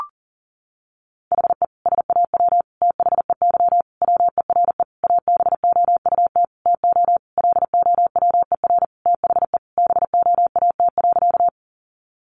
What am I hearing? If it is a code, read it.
Morse code, heard as a keyed tone, about 40 wpm, 700 Hz. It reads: HE SAW THEY WERE ABOUT TO LOWER THE BOAT.
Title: Sentences from Top 700 Words 40wpm